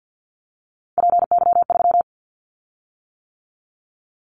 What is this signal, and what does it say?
Morse code, heard as a keyed tone, about 50 wpm, 700 Hz. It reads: PY3
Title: Callsign Prefixes for Top 100 DXCC - 50wpm